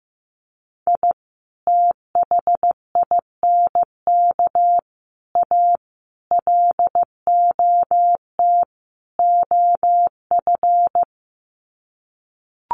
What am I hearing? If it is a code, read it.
Morse code, heard as a keyed tone, about 15 wpm, 700 Hz. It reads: I THINK A LOT OF
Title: Sets of 5 Words 15wpm